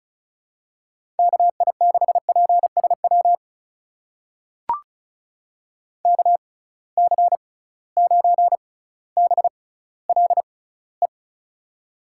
Morse code, heard as a keyed tone, about 36 wpm, spaced as for 18 wpm, 690 Hz. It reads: KI6PSW KC9BLE
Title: USA Call Signs - 6 Characters - 6x Character Spacing - 35wpm